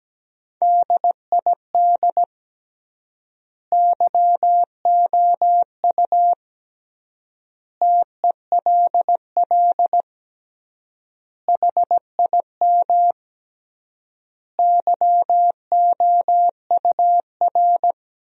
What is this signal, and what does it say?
Morse code, heard as a keyed tone, about 17 wpm, 700 Hz. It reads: DID YOU TELL HIM YOUR
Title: 3x Word-Spacing - Sentences from Top 400 Words - 17wpm